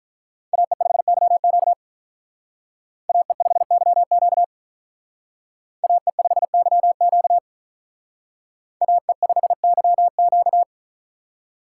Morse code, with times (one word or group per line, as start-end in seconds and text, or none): 0.5-1.7 AE5YQ
3.1-4.4 AE5YQ
5.8-7.4 AE5YQ
8.8-10.6 AE5YQ